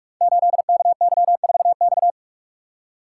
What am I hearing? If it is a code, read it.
Morse code, heard as a keyed tone, about 44 wpm, 700 Hz. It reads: 8KY4X
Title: Code Groups - 5 Letter:Numbers - 45wpm